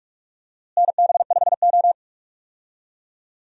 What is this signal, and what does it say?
Morse code, heard as a keyed tone, about 44 wpm, 700 Hz. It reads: NB5O
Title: USA Call Signs - 4 Characters - Repeated 3x - 45wpm